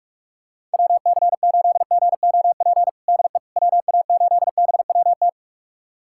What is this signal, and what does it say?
Morse code, heard as a keyed tone, about 41 wpm, 690 Hz. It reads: WC8GOP DE WA8BWT
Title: USA Callsign DE Callsign 45wpm